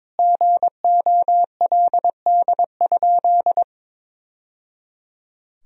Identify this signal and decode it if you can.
Morse code, heard as a keyed tone, about 22 wpm, 700 Hz. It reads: GOLD?
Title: Sentences from Top 1000 Words - Review Part 3 - 22wpm